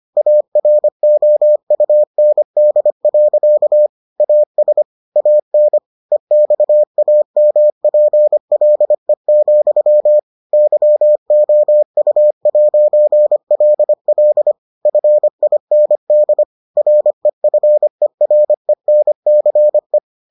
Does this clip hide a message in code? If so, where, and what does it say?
Yes: Morse code from 0.2 to 20.0 s, heard as a keyed tone, about 25 wpm, 600 Hz. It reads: AROUND. AS AN EXAMPLE, YOU'LL FIND REFERENCE